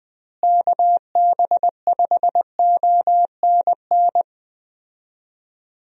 Morse code, heard as a keyed tone, about 21 wpm, 700 Hz. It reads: KB5ONN
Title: USA Call Signs - 6 Characters - Repeated 3x - 20wpm